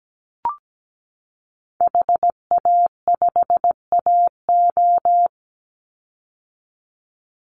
Morse code, heard as a keyed tone, about 17 wpm, 700 Hz. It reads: HA5AO
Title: Call Signs - 17wpm